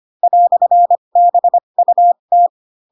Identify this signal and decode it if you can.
Morse code, heard as a keyed tone, about 25 wpm, spaced as for 22 wpm, 700 Hz. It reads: "BUT
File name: Pinocchio / Pinocchio ch7 18 25 ik1hge